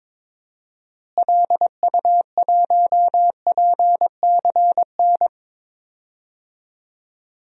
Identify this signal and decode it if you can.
Morse code, heard as a keyed tone, about 22 wpm, 700 Hz. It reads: LU1PCN
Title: DXCC-100 Call Signs - 6 Characters - Repeated 3x - 22wpm